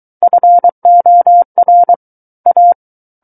Morse code, heard as a keyed tone, about 23 wpm, 700 Hz. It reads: FOR A
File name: Book One / 14-Chapter 14 - 23 WPM